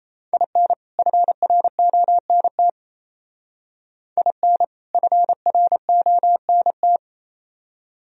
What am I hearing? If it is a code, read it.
Morse code, heard as a keyed tone, about 36 wpm, 700 Hz. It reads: IN FRONT IN FRONT